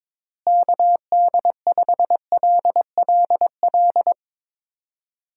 Morse code, heard as a keyed tone, about 21 wpm, 700 Hz. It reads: KD5LLL
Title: USA Call Signs - 6 Characters - Repeated 3x - 22wpm